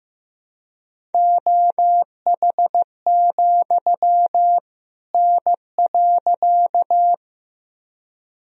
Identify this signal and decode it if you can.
Morse code, heard as a keyed tone, about 15 wpm, 700 Hz. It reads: OH, N.
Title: Sentences from Top 500 Words and Abbreviations 15wpm